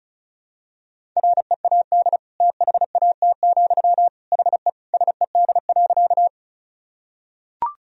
Morse code, heard as a keyed tone, about 34 wpm, 700 Hz. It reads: READ THAT, HE SED.
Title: Sentences from Top 200 Words and Abbreviations 35wpm